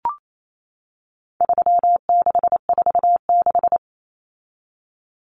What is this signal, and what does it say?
Morse code, heard as a keyed tone, about 28 wpm, 700 Hz. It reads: 3646
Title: Common Cut Numbers - 4-Digit 28wpm